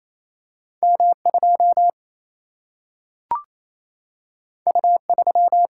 Morse code, heard as a keyed tone, about 28 wpm, 700 Hz. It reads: M 2 U 3
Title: Single-Letter Single-Number 28wpm